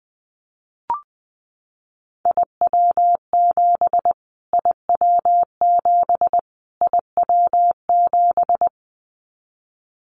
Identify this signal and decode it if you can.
Morse code, heard as a keyed tone, about 20 wpm, 700 Hz. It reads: IW7 IW7 IW7
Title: Callsign Prefixes for Top 25 DXCC - Repeated 3x - 20wpm